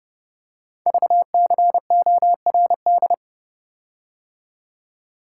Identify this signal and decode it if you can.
Morse code, heard as a keyed tone, about 29 wpm, 700 Hz. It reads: VCORD
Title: Code Groups - 5 Letters - 30wpm